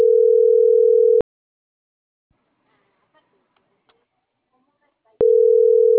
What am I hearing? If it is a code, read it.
Ringback tone (North America), 440 + 480 Hz.